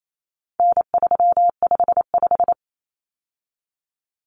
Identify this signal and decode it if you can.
Morse code, heard as a keyed tone, about 28 wpm, 700 Hz. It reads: N355
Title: Common Cut Numbers - 4-Digit 28wpm